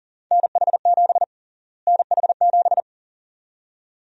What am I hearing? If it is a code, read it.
Morse code, heard as a keyed tone, about 40 wpm, 700 Hz. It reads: NH7 NH7
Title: USA Call Sign Prefixes - Repeated 2x - 40wpm